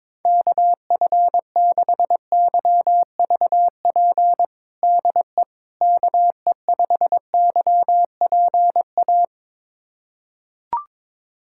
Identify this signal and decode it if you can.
Morse code, heard as a keyed tone, about 22 wpm, 700 Hz. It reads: KF6YVP DE KE5YPA
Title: USA Callsign DE Callsign 22wpm